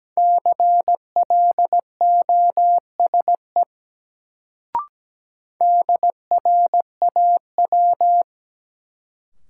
Morse code, heard as a keyed tone, about 17 wpm, 700 Hz. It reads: CLOSE DRAW
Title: Sentences from Top 300 Words - New Word Drill - 17wpm